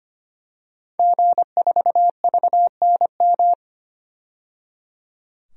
Morse code, heard as a keyed tone, about 25 wpm, 700 Hz. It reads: G4VNM
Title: DXCC-100 Call Signs - 5 Characters - 25wpm